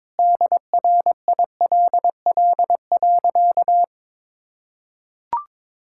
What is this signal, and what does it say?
Morse code, heard as a keyed tone, about 22 wpm, 700 Hz. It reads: DRILL.
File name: practice / 22wpm / Sentences with 3-Word Phrases - 22wpm